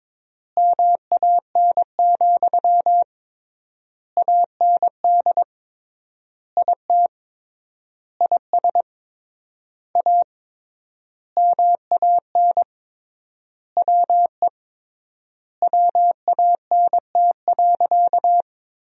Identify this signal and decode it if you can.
Morse code, heard as a keyed tone, about 22 wpm, 700 Hz. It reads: MAN, AND IT IS A MAN WE WANT.
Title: 3x Word-Spacing - Sentences from Top 200 Words - 22wpm